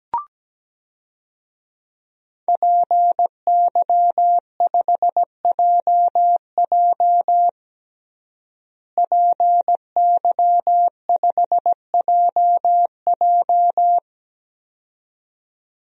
Morse code, heard as a keyed tone, about 17 wpm, 700 Hz. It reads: PY5JJ PY5JJ
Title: DXCC-100 Call Signs - 5 Characters - Repeated 2x - 17wpm